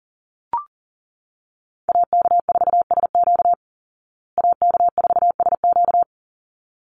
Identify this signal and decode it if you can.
Morse code, heard as a keyed tone, about 39 wpm, 700 Hz. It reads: AK4SQ AK4SQ
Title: Call Signs Repeated 3x - 40wpm